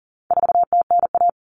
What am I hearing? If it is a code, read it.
Morse code, heard as a keyed tone, about 40 wpm, 700 Hz. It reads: 4TNA
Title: Common Cut Numbers - 4-Digit 40wpm